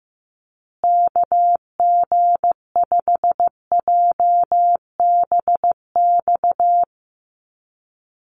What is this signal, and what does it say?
Morse code, heard as a keyed tone, about 15 wpm, 700 Hz. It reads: KG5JBX
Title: USA Call Signs Repeated 3x 15wpm